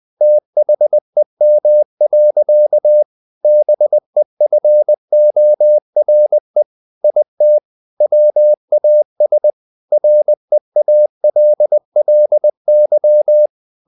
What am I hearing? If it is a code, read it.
Morse code, heard as a keyed tone, about 20 wpm, 600 Hz. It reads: THEM. BEFORE IT WAS REALLY
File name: the old man and the sea / the old man and the sea part 1 20wpm